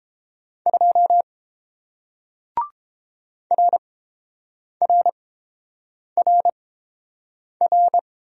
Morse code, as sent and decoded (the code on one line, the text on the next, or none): ..--- / .-. .-. .-. .-.
2 RRRR